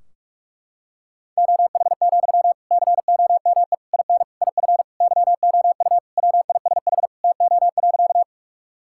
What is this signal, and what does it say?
Morse code, heard as a keyed tone, about 46 wpm, 700 Hz. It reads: OH, COME IN IF YOU WISH TO.
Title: Sentences from Top 1600 Words 45wpm